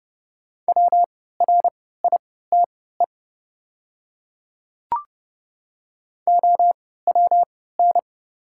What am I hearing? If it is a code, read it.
Morse code, heard as a keyed tone, about 30 wpm, spaced as for 17 wpm, 690 Hz. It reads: WRITE OWN
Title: 3x Character Spacing - Top 300 Words - 30wpm